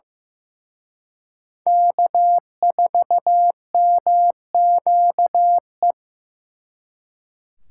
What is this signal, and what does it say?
Morse code, heard as a keyed tone, about 18 wpm, 700 Hz. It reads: K4MQE
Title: Speed Racing - USA 1x3 Call Signs 15wpm